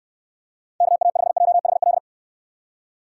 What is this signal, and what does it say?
Morse code, heard as a keyed tone, about 69 wpm, 700 Hz. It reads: KA5JHL